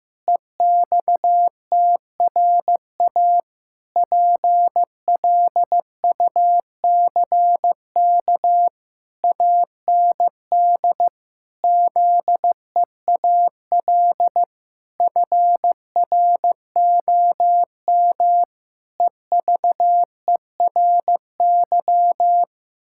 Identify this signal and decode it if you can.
Morse code, heard as a keyed tone, about 15 wpm, 700 Hz. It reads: EXTRA PLUCK AND ZEAL FROM EVERY
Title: Pangram Challenge v2 - 15wpm